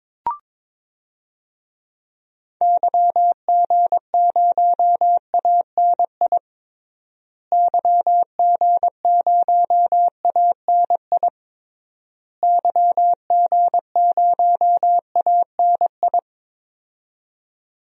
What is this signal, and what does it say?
Morse code, heard as a keyed tone, about 22 wpm, 700 Hz. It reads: YG0ANI YG0ANI YG0ANI
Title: DXCC-100 Call Signs - 6 Characters - Repeated 3x - 22wpm